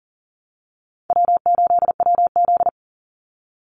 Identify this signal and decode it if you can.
Morse code, heard as a keyed tone, about 39 wpm, 700 Hz. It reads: W8WZ
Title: Easy Call Signs Repeated 3x - 40wpm